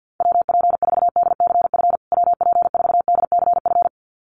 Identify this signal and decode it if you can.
Morse code, heard as a keyed tone, about 50 wpm, 700 Hz. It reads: WP4DCF WP4DCF
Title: USA Call Signs Repeated 2x 50wpm